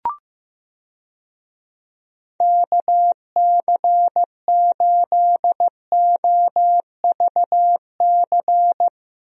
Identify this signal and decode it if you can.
Morse code, heard as a keyed tone, about 15 wpm, 700 Hz. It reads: KC8OVC
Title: USA Call Signs - 6 Characters - 15wpm